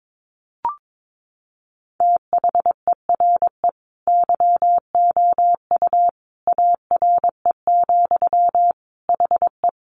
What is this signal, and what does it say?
Morse code, heard as a keyed tone, about 22 wpm, 700 Hz. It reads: THERE YOU ARE, HE
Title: Sentences from Top 100 Words 22wpm